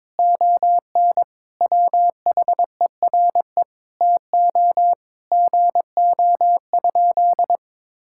Morse code, heard as a keyed tone, about 21 wpm, 700 Hz. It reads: ON WHERE TO GO?
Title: Sentences from Top 1600 Words 22wpm